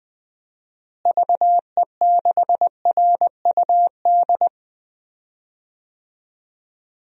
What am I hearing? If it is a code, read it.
Morse code, heard as a keyed tone, about 20 wpm, 700 Hz. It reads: VE6RUD
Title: DXCC-100 Call Signs - 6 Characters - Repeated 3x - 20wpm